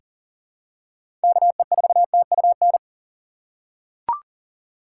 Morse code, heard as a keyed tone, about 37 wpm, 700 Hz. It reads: KE4TUN
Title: USA Call Signs - 6 Characters - 4x Character Spacing - 40wpm